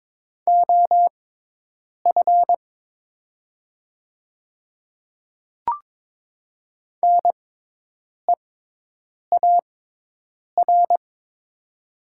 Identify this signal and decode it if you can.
Morse code, heard as a keyed tone, about 22 wpm, spaced as for 8 wpm, 700 Hz. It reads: OF NEAR
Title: 6x Character Spacing - Top 200 Words - 22wpm